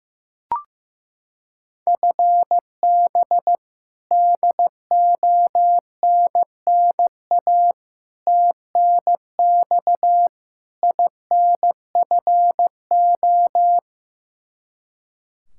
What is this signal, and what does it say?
Morse code, heard as a keyed tone, about 16 wpm, 700 Hz. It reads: FB DONNA TNX INFO
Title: Practice QSO - Exchange 3 - More Info - 15wpm